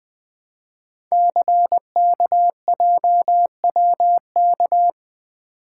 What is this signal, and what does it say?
Morse code, heard as a keyed tone, about 21 wpm, 700 Hz. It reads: CKJWK
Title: Code Groups - 5 Letters - 20wpm